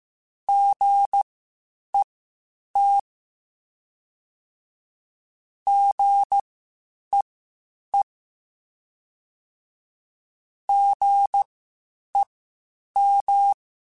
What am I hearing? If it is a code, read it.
Morse code, heard as a keyed tone, about 15 wpm, spaced as for 8 wpm, 800 Hz. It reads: GET GEE GEM
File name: Spaced Alphas / Spaced Alphas mp3s / Gs sp @13